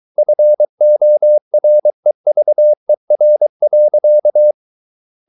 Morse code, heard as a keyed tone, about 23 wpm, 600 Hz. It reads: FOREVER.